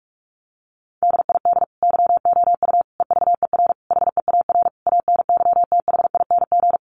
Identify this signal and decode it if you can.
Morse code, heard as a keyed tone, about 44 wpm, 700 Hz. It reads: DID YOU EVER HEAR ANYTHING
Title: Sentences from Top 1400 Words 45wpm